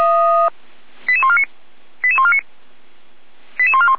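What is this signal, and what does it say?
Five-tone selective call (ZVEI), heard as quick runs of five tones, tones of about 70 ms. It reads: 88149 88149 88141
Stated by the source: ZVEI five-tone selective call received on an SDR.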